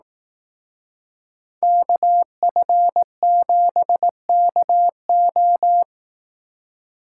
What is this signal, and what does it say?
Morse code, heard as a keyed tone, about 18 wpm, 700 Hz. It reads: KF7KO
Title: Speed Racing - USA 2x2 Call Signs 15wpm